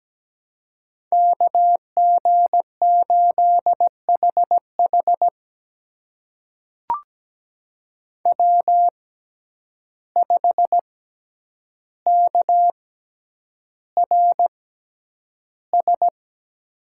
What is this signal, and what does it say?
Morse code, heard as a keyed tone, about 17 wpm, spaced as for 9 wpm, 700 Hz. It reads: KG8HH W5KRS
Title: USA Call Signs - 5 Characters - 6x Character Spacing - 17wpm